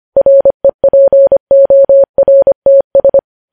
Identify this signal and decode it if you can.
Morse code, heard as a keyed tone, about 25 wpm, 550 Hz. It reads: REPORTS
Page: News Headlines in Morse Code at 25 WPM